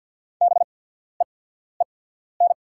Morse code, heard as a keyed tone, about 48 wpm, spaced as for 14 wpm, 690 Hz. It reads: BEEN
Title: 8x Character Spacing - Top 200 Words - 50wpm